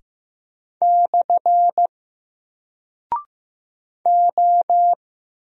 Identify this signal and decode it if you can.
Morse code, heard as a keyed tone, about 15 wpm, 700 Hz. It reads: / O
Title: Single Character (Letter:Number:Punctuation:Prosign) 15wpm